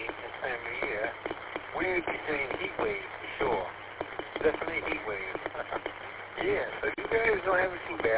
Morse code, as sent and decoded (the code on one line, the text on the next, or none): none